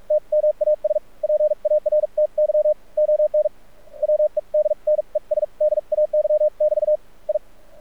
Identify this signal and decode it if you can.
Morse code, heard as a keyed tone, about 46 wpm, spaced as for 38 wpm, 610 Hz. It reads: TMAS PARTY ON WEDNESDAY <BT> I
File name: archived / CRSnet